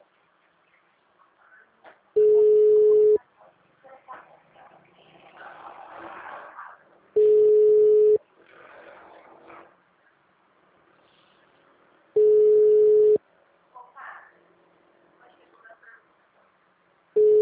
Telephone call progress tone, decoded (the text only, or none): ringback tone